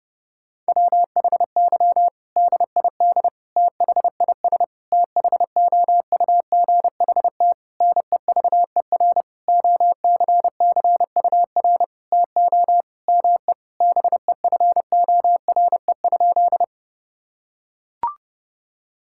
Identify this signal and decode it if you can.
Morse code, heard as a keyed tone, about 32 wpm, 700 Hz. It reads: WHY DID THIS THOUGHT NEVER OCCUR TO ME BEFORE?
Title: Sentences from Top 1000 Words - Review Part 6 - 30wpm